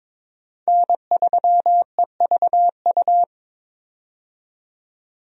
Morse code, heard as a keyed tone, about 22 wpm, 700 Hz. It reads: N3EVU
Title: USA Call Signs - 5 Characters - 22wpm